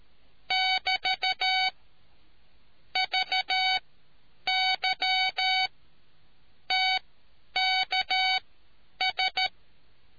Morse code, heard as a keyed tone, about 13 wpm, spaced as for 9 wpm, 750 Hz. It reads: <BT> VY TKS